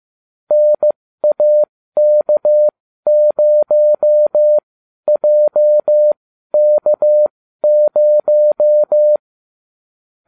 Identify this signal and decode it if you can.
Morse code, heard as a keyed tone, about 15 wpm, spaced as for 12 wpm, 600 Hz. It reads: NAK0JK0